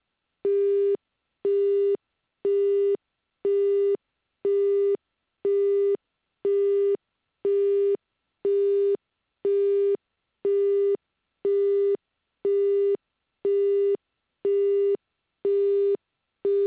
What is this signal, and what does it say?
Busy tone (Japan), 400 Hz.
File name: japanese